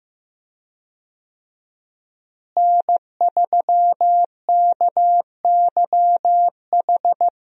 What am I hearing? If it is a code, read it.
Morse code, heard as a keyed tone, about 15 wpm, 700 Hz. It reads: N3KYH
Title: USA Call Signs - 5 Characters - Repeated 2x - 15wpm